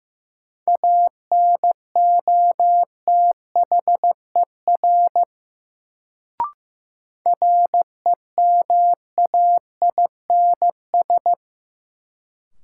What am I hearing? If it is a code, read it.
Morse code, heard as a keyed tone, about 15 wpm, 700 Hz. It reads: ANOTHER REMAINS
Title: Sets of 5 Words Encore - New Word Drill - 15wpm